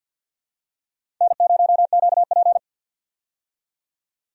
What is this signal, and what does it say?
Morse code, heard as a keyed tone, about 49 wpm, 700 Hz. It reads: N0QP
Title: USA Call Signs - 4 Characters - 50wpm